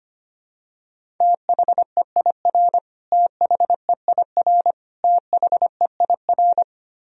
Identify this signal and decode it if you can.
Morse code, heard as a keyed tone, about 25 wpm, 700 Hz. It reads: THEIR THEIR THEIR
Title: Top 100 Words Repeated 25wpm